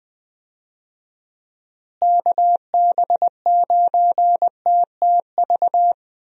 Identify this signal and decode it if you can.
Morse code, heard as a keyed tone, about 20 wpm, 700 Hz. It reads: KB9TTV
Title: USA Call Signs - 6 Characters - Repeated 3x - 20wpm